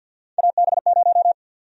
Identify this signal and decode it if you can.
Morse code, heard as a keyed tone, about 53 wpm, 700 Hz. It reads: AB0
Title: USA Call Sign Prefixes - Repeated 2x - 50wpm